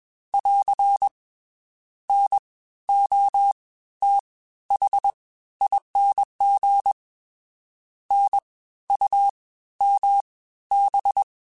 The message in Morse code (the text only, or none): <AR> NOTHING NUMB